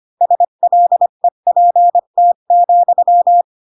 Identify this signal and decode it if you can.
Morse code, heard as a keyed tone, about 25 wpm, 700 Hz. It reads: SLEPT,